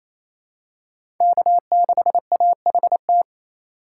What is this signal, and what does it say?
Morse code, heard as a keyed tone, about 28 wpm, 700 Hz. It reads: K6AHT
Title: USA Call Signs - 5 Characters - 2x Character Spacing - 28wpm